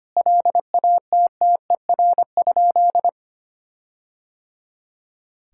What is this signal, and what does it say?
Morse code, heard as a keyed tone, about 25 wpm, 700 Hz. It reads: LATTER?
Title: Sentences from Top 1600 Words 25wpm